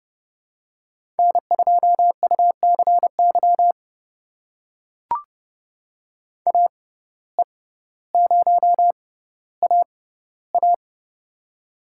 Morse code, heard as a keyed tone, about 30 wpm, spaced as for 15 wpm, 700 Hz. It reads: N2UCY AE0AA
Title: USA Call Signs - 5 Characters - 6x Character Spacing - 30wpm